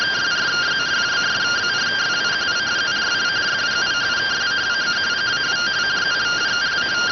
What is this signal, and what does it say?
Two-tone teleprinter signal, about 170 Hz shift.